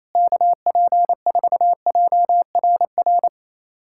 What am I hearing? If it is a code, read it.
Morse code, heard as a keyed tone, about 28 wpm, 700 Hz. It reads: KP4JRR
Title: USA Callsign DE Callsign - 6 Characters - 28wpm